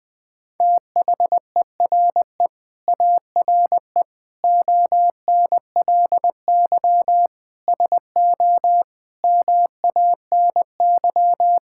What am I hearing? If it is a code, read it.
Morse code, heard as a keyed tone, about 20 wpm, 700 Hz. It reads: THERE ARE ONLY SO MANY
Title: Sets of 9 Words - 20wpm